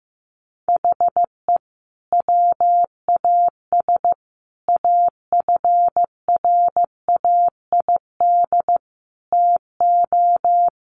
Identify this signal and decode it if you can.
Morse code, heard as a keyed tone, about 15 wpm, 700 Hz. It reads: HE WAS AFRAID TO
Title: Sentences from Top 1500 Words 15wpm